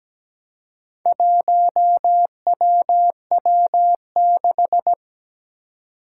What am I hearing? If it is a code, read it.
Morse code, heard as a keyed tone, about 17 wpm, 700 Hz. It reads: 1WW6
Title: Four Challenging Letter-Number Combinations 17wpm